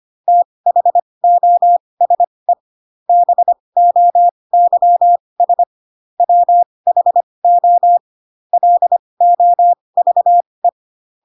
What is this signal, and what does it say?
Morse code, heard as a keyed tone, about 25 wpm, spaced as for 20 wpm, 700 Hz. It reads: THOSE BOYS WHO LOVE
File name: Pinocchio / Pinocchio ch6 15 25 ik1hge